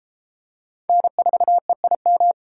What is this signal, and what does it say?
Morse code, heard as a keyed tone, about 31 wpm, 700 Hz. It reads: N4EIM